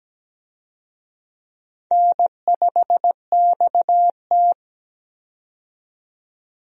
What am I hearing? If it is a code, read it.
Morse code, heard as a keyed tone, about 17 wpm, 700 Hz. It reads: N5XT
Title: USA Call Signs - 4 Characters - 17wpm